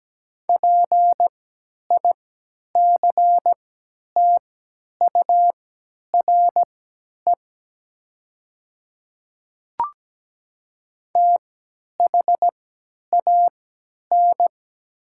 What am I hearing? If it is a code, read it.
Morse code, heard as a keyed tone, about 17 wpm, spaced as for 10 wpm, 700 Hz. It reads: PICTURE THAN
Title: 3x Character Spacing - Top 300 Words - 17wpm